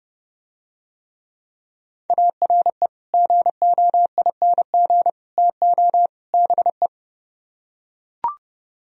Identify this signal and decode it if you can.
Morse code, heard as a keyed tone, about 30 wpm, 700 Hz. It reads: ARE GOING TO BE